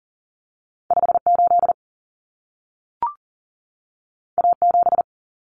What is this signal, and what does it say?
Morse code, heard as a keyed tone, about 40 wpm, 700 Hz. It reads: 58 A7
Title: Common Cut Numbers - 2-Digit 40wpm